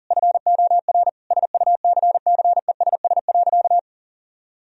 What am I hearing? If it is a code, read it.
Morse code, heard as a keyed tone, about 40 wpm, 700 Hz. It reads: FOR SUCCESS.
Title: Sentences from Top 1000 Words - Review Part 3 - 40wpm